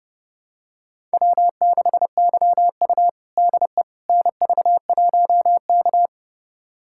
Morse code, heard as a keyed tone, about 30 wpm, 700 Hz. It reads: W6YU DE NV1K
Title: USA Callsign DE Callsign - 4 Characters - 30wpm